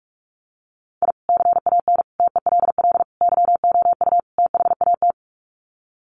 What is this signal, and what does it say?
Morse code, heard as a keyed tone, about 44 wpm, 700 Hz. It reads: I CAN TELL YOU THAT
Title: Sets of 5 Words 45wpm